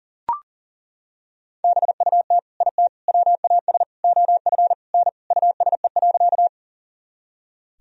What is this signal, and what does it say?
Morse code, heard as a keyed tone, about 37 wpm, 700 Hz. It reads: BUT IT WAS OF N USE.
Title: Sentences from Top 100 Words and Abbreviations 40wpm